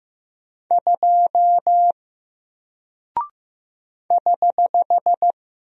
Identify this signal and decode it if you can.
Morse code, heard as a keyed tone, about 15 wpm, 700 Hz. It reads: 2 <HH>
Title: Single Character (Letter:Number:Punctuation:Prosign) Alt 15wpm